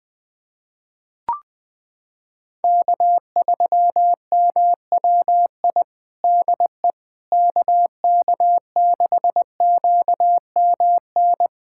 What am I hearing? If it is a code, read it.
Morse code, heard as a keyed tone, about 21 wpm, 700 Hz. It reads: K3MWI DE KK6QMN
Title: USA Callsign DE Callsign 20wpm